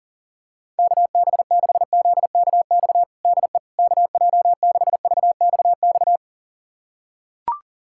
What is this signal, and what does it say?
Morse code, heard as a keyed tone, about 40 wpm, 690 Hz. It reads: KB6ZKX DE KJ6VXX
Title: USA Callsign DE Callsign - 6 Characters - 40wpm